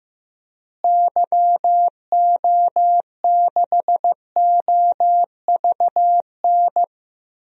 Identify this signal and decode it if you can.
Morse code, heard as a keyed tone, about 15 wpm, 700 Hz. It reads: YO6OVN
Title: DXCC-100 Call Signs - 6 Characters - Repeated 2x - 15wpm